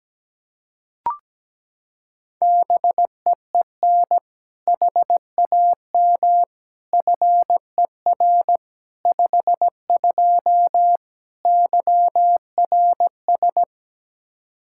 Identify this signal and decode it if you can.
Morse code, heard as a keyed tone, about 17 wpm, 700 Hz. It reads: BEEN HAM FER 52 YRS
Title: Practice QSO - Exchange 3 - More Info - 17wpm